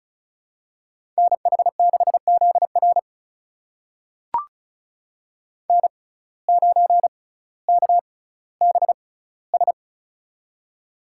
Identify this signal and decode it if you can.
Morse code, heard as a keyed tone, about 36 wpm, spaced as for 18 wpm, 700 Hz. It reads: NH6ZR N9KBS
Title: USA Call Signs - 5 Characters - 6x Character Spacing - 35wpm